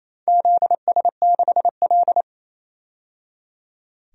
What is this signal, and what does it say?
Morse code, heard as a keyed tone, about 28 wpm, 700 Hz. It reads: ZS6L
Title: DXCC-100 Call Signs - 4 Characters - 28wpm